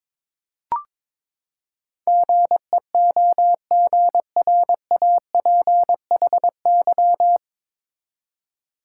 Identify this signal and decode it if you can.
Morse code, heard as a keyed tone, about 22 wpm, 700 Hz. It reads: GEOGRAPHY
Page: Morse Code Ninja practice 22wpm